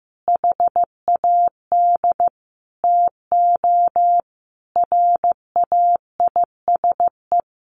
Morse code, heard as a keyed tone, about 14 wpm, 700 Hz. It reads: HAD TO RAISE
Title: Sentences from Top 1400 Words 15wpm